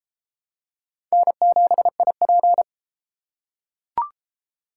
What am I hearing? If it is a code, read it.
Morse code, heard as a keyed tone, about 31 wpm, 700 Hz. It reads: N7IP